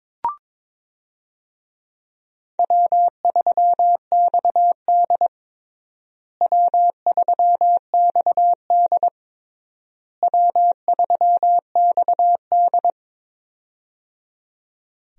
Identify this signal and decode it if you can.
Morse code, heard as a keyed tone, about 22 wpm, 700 Hz. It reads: W3XD W3XD W3XD
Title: USA Call Signs - 4 Characters - Repeated 3x - 22wpm